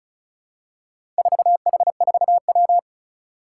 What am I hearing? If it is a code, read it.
Morse code, heard as a keyed tone, about 34 wpm, 700 Hz. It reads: 4H4W
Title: Four Challenging Letter-Number Combinations 35wpm